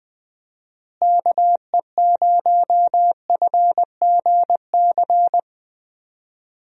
Morse code, heard as a keyed tone, about 20 wpm, 700 Hz. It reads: KE0FGC
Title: USA Call Signs - 6 Characters - Repeated 3x - 20wpm